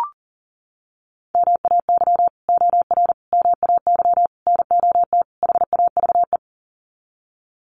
Morse code, heard as a keyed tone, about 37 wpm, 700 Hz. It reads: MAY OR MAY NOT HAVE
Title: Sets of 5 Words Encore2 40wpm